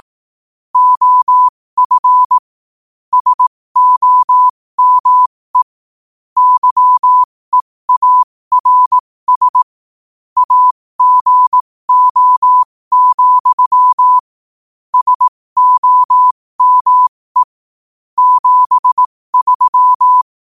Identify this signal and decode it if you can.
Morse code, heard as a keyed tone, about 18 wpm, spaced as for 15 wpm, 1000 Hz. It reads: OF SOME YEARS AGO, SOME 73
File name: qotdfiles / 15wpm